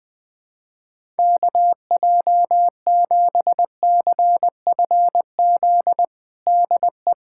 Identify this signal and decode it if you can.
Morse code, heard as a keyed tone, about 20 wpm, 700 Hz. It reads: KJ7CFZ DE
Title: USA Callsign DE Callsign - 6 Characters - 20wpm